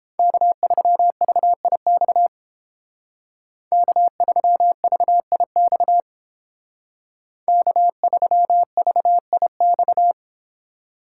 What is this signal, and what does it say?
Morse code, heard as a keyed tone, about 31 wpm, 700 Hz. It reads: K3VIX K3VIX K3VIX